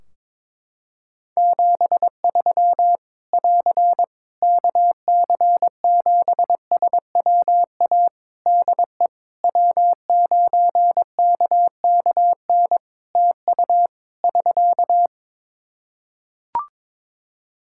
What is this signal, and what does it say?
Morse code, heard as a keyed tone, about 22 wpm, 700 Hz. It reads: 73 <AR> KC7SWA DE W9KKN TU <SK>
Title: Practice QSO - Exchange 4 - Ending - 22wpm